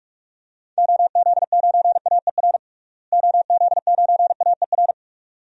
Morse code, heard as a keyed tone, about 44 wpm, 700 Hz. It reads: OZ9AER OZ9AER
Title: DXCC-100 Call Signs - 6 Characters - Repeated 2x - 45wpm